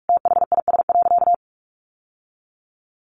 Morse code, heard as a keyed tone, about 44 wpm, 700 Hz. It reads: THIS.
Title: Sentences from Top 1200 Words 45wpm